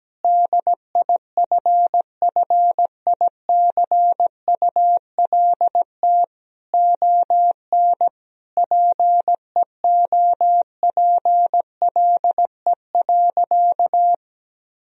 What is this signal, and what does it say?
Morse code, heard as a keyed tone, about 17 wpm, 700 Hz. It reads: DIFFICULT ON PEOPLE.
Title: Sentences from Top 1000 Words - Review Part 3 - 17wpm